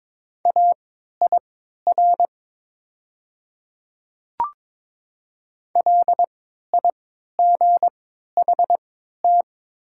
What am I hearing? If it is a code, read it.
Morse code, heard as a keyed tone, about 22 wpm, spaced as for 13 wpm, 700 Hz. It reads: AIR LIGHT
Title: 3x Character Spacing - Top 300 Words - 22wpm